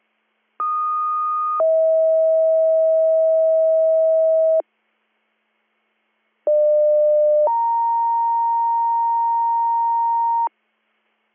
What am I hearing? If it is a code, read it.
Two-tone page, heard as a one-second tone, then a three-second tone, about 1230 then 640 Hz, about 600 then 930 Hz. It is heard from 0.6 to 10.5 s.